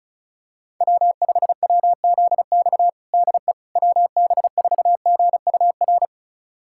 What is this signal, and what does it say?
Morse code, heard as a keyed tone, about 36 wpm, 700 Hz. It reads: W5WZX DE WB4GUR
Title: USA Callsign DE Callsign 35wpm